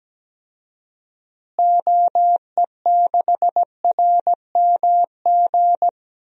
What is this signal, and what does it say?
Morse code, heard as a keyed tone, about 17 wpm, 700 Hz. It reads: OE6RMG
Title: DXCC-100 Call Signs - 6 Characters - Repeated 2x - 17wpm